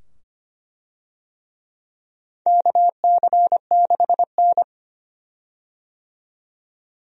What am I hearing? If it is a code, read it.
Morse code, heard as a keyed tone, about 25 wpm, 700 Hz. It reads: KC6N
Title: USA Call Signs - 4 Characters - 25wpm